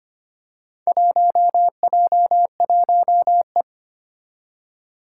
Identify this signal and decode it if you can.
Morse code, heard as a keyed tone, about 25 wpm, 700 Hz. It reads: 1J1E
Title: Four Challenging Letter-Number Combinations 25wpm